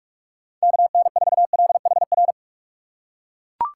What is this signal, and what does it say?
Morse code, heard as a keyed tone, about 41 wpm, 700 Hz. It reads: KN4LHR